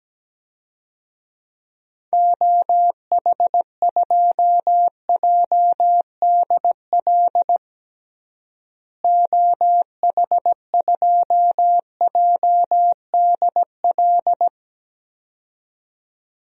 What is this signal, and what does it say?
Morse code, heard as a keyed tone, about 17 wpm, 700 Hz. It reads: OH2JDL OH2JDL
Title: DXCC-100 Call Signs - 6 Characters - Repeated 2x - 17wpm